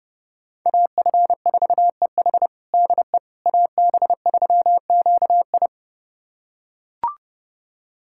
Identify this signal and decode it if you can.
Morse code, heard as a keyed tone, about 32 wpm, 700 Hz. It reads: AF4EH DE AB3QI
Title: USA Callsign DE Callsign - 5 Characters - 30wpm